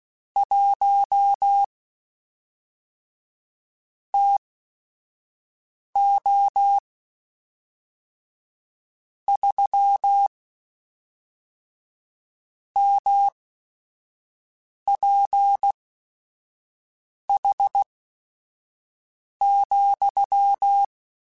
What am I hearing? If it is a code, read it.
Morse code, heard as a keyed tone, about 16 wpm, spaced as for 5 wpm, 800 Hz. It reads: 1 TO 3 MPH,
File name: Weather Sets / Weather Sets-mp3s / Weather 5 5 @5